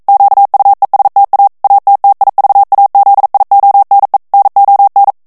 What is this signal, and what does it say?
Morse code, heard as a keyed tone, about 42 wpm, 800 Hz. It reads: QUESTA ATTIVAZIONE NON
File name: morse mp3 / ITALIEN texte clair 1 vitesse 40 mots minute